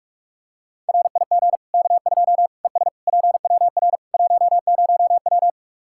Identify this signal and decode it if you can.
Morse code, heard as a keyed tone, about 46 wpm, 700 Hz. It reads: RIG K2 ES PWR 10W
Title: Practice QSO - Exchange 2 - Info - 45wpm